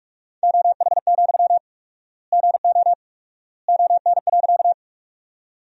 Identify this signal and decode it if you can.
Morse code, heard as a keyed tone, about 46 wpm, 700 Hz. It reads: OH, GO ON.
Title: 4x Word-Spacing - Sentences from Top 500 Words - 45wpm